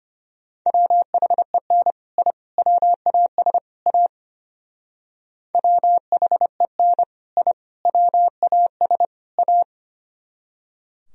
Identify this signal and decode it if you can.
Morse code, heard as a keyed tone, about 31 wpm, 700 Hz. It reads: WHEN I WAS A WHEN I WAS A
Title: Speed-Racing - Sets of 4 Words 25wpm